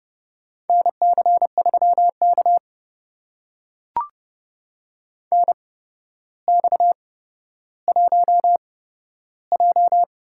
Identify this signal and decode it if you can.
Morse code, heard as a keyed tone, about 30 wpm, spaced as for 13 wpm, 700 Hz. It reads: NC3K NX1J
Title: USA Call Signs - 4 Characters - 8x Character Spacing - 30wpm